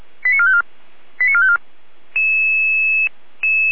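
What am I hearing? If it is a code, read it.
Five-tone selective call (ZVEI), heard as quick runs of five tones, tones of about 70 ms. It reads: 89454 89454
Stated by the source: ZVEI five-tone selective call received on SDR, Feuerwehr Orlamünde.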